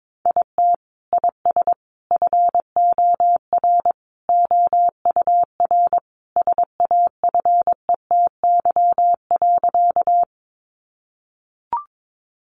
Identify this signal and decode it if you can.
Morse code, heard as a keyed tone, about 22 wpm, 700 Hz. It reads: IT IS FOR OUR SAFETY.
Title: Sentences from Top 700 Words 22wpm